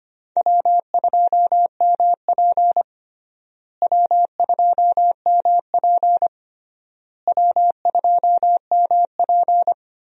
Morse code, heard as a keyed tone, about 25 wpm, 700 Hz. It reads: W2MP W2MP W2MP
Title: USA Call Signs - 4 Characters - Repeated 3x - 25wpm